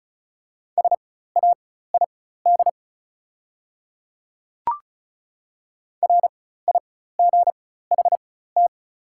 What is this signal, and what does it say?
Morse code, heard as a keyed tone, about 36 wpm, spaced as for 16 wpm, 700 Hz. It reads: SAID RIGHT
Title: 4x Character Spacing - Top 200 Words - 35wpm